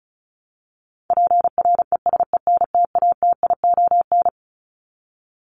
Morse code, heard as a keyed tone, about 34 wpm, 700 Hz. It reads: PRESENTATION
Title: Top 1501-1750 Words 35wpm